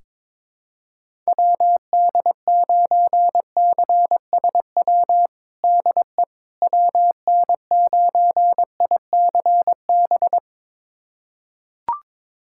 Morse code, heard as a keyed tone, about 22 wpm, 700 Hz. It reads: WD9CSW DE WN9ICB
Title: USA Callsign DE Callsign 22wpm